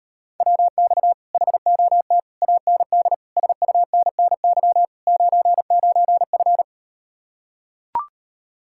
Morse code, heard as a keyed tone, about 38 wpm, 700 Hz. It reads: WX HOT AND SUNNY 99F